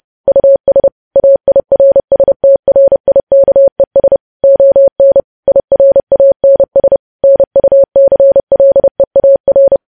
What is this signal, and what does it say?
Morse code, heard as a keyed tone, about 30 wpm, 550 Hz. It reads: US AIRSTRIKES ON IRANS NUCLEAR
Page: news headlines in morse code at 30 wpm